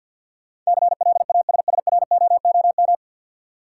Morse code, heard as a keyed tone, about 52 wpm, 700 Hz. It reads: CLASSROOM
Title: Sentences from Top 1500 Words - New Word Drill Alt - 50wpm